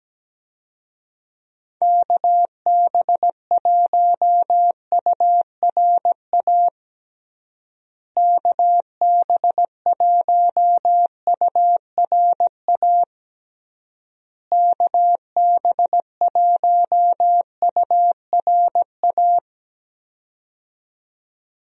Morse code, heard as a keyed tone, about 17 wpm, 700 Hz. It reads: KB1URA KB1URA KB1URA
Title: USA Call Signs - 6 Characters - Repeated 3x - 17wpm